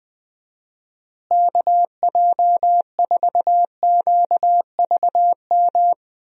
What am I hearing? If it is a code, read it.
Morse code, heard as a keyed tone, about 20 wpm, 700 Hz. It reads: KJ4QVM
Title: USA Call Signs - 6 Characters - Repeated 2x - 20wpm